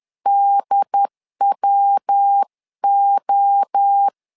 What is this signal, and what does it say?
Morse code, heard as a keyed tone, about 10 wpm, 800 Hz. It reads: DWO
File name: LEZIONI / 4 Q1S3 10wpm 2427